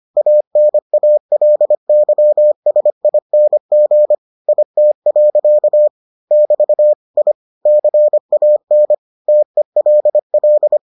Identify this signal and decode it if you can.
Morse code, heard as a keyed tone, about 25 wpm, 600 Hz. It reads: ANALYSING IT. <BT> I CAN TELL